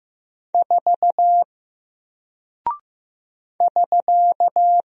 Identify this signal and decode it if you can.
Morse code, heard as a keyed tone, about 15 wpm, 700 Hz. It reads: 4 <SK>
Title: Single Character (Letter:Number:Punctuation:Prosign) Alt 15wpm